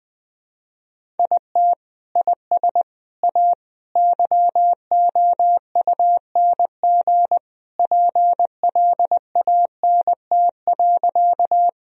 Morse code, heard as a keyed tone, about 20 wpm, 700 Hz. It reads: IT IS A YOUNG PLANT.
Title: Sentences from Top 1000 Words - Review Part 1 - 20wpm